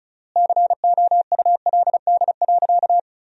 Morse code, heard as a keyed tone, about 36 wpm, 700 Hz. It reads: COULD.
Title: 3x Word-Spacing - Sentences from Top 500 Words - 35wpm